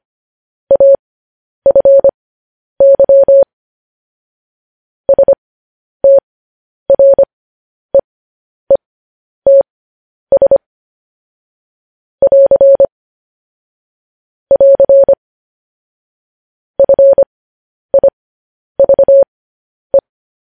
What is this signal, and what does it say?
Morse code, heard as a keyed tone, about 25 wpm, spaced as for 10 wpm, 550 Hz. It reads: AFY STREETS <AR> <AR> FIVE
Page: news-headlines-in-morse-code-at-10-wpm